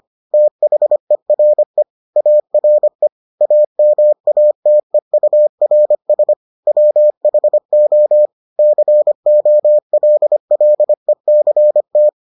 Morse code, heard as a keyed tone, about 25 wpm, 600 Hz. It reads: THERE ARE AMATEURS WHO COLLECT